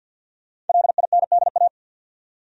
Morse code, heard as a keyed tone, about 50 wpm, 700 Hz. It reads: LINDA
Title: USA Common Names - Max 5 Letters - 50wpm